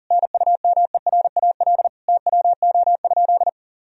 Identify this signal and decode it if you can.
Morse code, heard as a keyed tone, about 40 wpm, 700 Hz. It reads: NUMERAL TWO?
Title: Sentences from Top 1000 Words - Review Part 2 - 40wpm